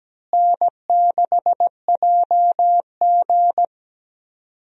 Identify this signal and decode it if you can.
Morse code, heard as a keyed tone, about 17 wpm, 700 Hz. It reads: N6JG
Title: USA Call Signs - 4 Characters - Repeated 2x - 17wpm